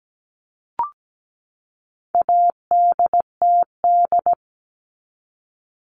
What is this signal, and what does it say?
Morse code, heard as a keyed tone, about 17 wpm, 700 Hz. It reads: ADTD
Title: All Cut Numbers - 4-Digit 17wpm